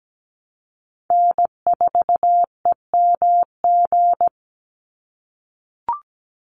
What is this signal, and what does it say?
Morse code, heard as a keyed tone, about 17 wpm, 700 Hz. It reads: N4EMG
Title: NA SOTA Chaser Callsigns 17wpm